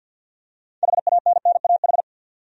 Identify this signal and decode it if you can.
Morse code, heard as a keyed tone, about 52 wpm, 700 Hz. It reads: HANNAH
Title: USA Common Names - 50wpm